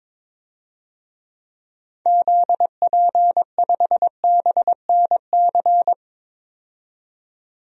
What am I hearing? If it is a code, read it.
Morse code, heard as a keyed tone, about 22 wpm, 700 Hz. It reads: ZP5BNC
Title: DXCC-100 Call Signs - 6 Characters - 22wpm